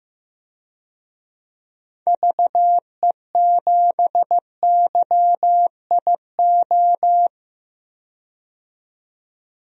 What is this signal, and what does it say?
Morse code, heard as a keyed tone, about 15 wpm, 700 Hz. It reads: VE7YIO
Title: DXCC-100 Call Signs - 6 Characters - 15wpm